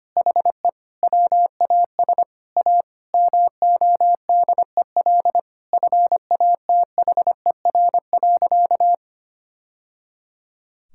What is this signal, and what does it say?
Morse code, heard as a keyed tone, about 25 wpm, 700 Hz. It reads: HE WAS A MODEL FATHER.
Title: Sentences from Top 1000 Words - Review Part 5 - 25wpm